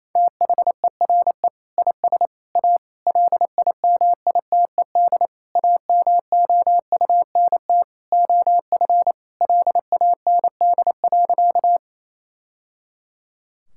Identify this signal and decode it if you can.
Morse code, heard as a keyed tone, about 28 wpm, 700 Hz. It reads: THERE IS A LIMITED AMOUNT OF LAND.
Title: Sentences from Top 1000 Words - Review Part 6 - 28wpm